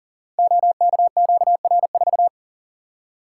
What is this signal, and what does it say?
Morse code, heard as a keyed tone, about 39 wpm, 700 Hz. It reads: OKQR4
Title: Code Groups - 5 Letter:Numbers - 40wpm